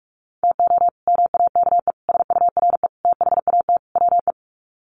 Morse code, heard as a keyed tone, about 44 wpm, 700 Hz. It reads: TO MAKE SURE THAT WE
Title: Sets of 5 Words Encore 45wpm